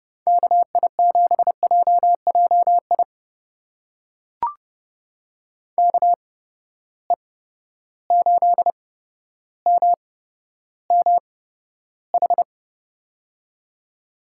Morse code, heard as a keyed tone, about 30 wpm, spaced as for 13 wpm, 700 Hz. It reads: KI7JJI KE8MMH
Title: USA Call Signs - 6 Characters - 8x Character Spacing - 30wpm